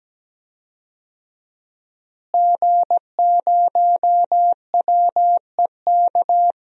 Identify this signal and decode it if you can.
Morse code, heard as a keyed tone, about 17 wpm, 700 Hz. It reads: G0WEK
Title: DXCC-100 Call Signs - 5 Characters - Repeated 2x - 17wpm